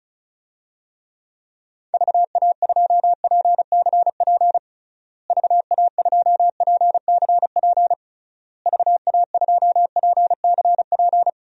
Morse code, heard as a keyed tone, about 36 wpm, 700 Hz. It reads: VA2PCP VA2PCP VA2PCP
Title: DXCC-100 Call Signs - 6 Characters - Repeated 3x - 35wpm